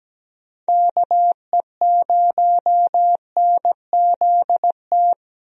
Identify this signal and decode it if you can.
Morse code, heard as a keyed tone, about 17 wpm, 700 Hz. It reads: KE0NZT
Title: USA Call Signs - 6 Characters - 17wpm